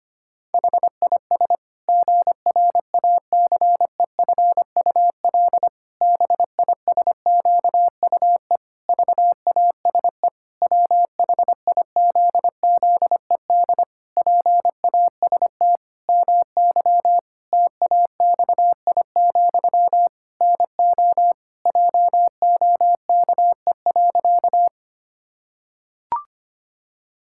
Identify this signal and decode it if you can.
Morse code, heard as a keyed tone, about 25 wpm, 700 Hz. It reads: HIS GRACEFUL BISQUE VASE WHIZZED PAST MY TAXI, NO JOKE.
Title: Pangram Challenge v2 - 25wpm